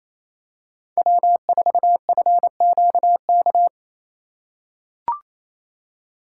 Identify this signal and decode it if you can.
Morse code, heard as a keyed tone, about 27 wpm, 700 Hz. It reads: W4FQK
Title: USA Call Signs - 5 Characters - 4x Character Spacing - 28wpm